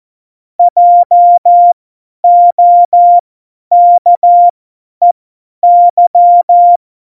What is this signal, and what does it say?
Morse code, heard as a keyed tone, about 14 wpm, spaced as for 10 wpm, 700 Hz. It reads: JOKEY